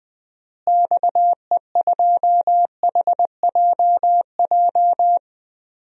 Morse code, heard as a keyed tone, about 20 wpm, 700 Hz. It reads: XE2HJJ
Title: DXCC-100 Call Signs - 6 Characters - Repeated 3x - 20wpm